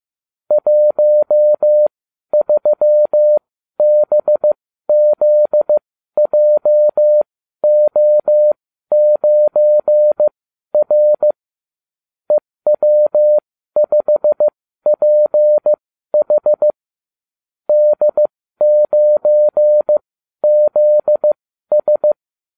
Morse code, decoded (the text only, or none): 13BZJO9R EW5PH D9ZS